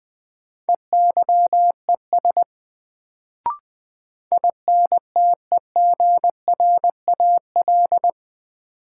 Morse code, heard as a keyed tone, about 20 wpm, 700 Hz. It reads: EYES INTEGRAL
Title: Sets of 5 Words - New Word Drill - 20wpm